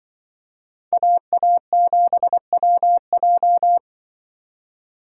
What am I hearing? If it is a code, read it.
Morse code, heard as a keyed tone, about 23 wpm, 700 Hz. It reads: AA7WJ
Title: Speed Racing - USA 2x2 Call Signs 20wpm